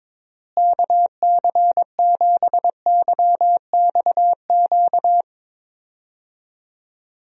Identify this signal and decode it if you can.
Morse code, heard as a keyed tone, about 22 wpm, 700 Hz. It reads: KC7YXQ
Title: USA Call Signs - 6 Characters - Repeated 3x - 22wpm